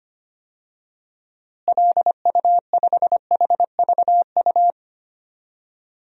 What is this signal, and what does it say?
Morse code, heard as a keyed tone, about 25 wpm, 700 Hz. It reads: LU5HVU
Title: DXCC-100 Call Signs - 6 Characters - Repeated 2x - 25wpm